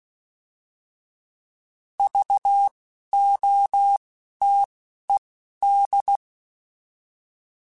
Morse code, heard as a keyed tone, about 16 wpm, spaced as for 11 wpm, 800 Hz. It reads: VOTED
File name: Alphabetical / Alphabetical mp3s / Vs @10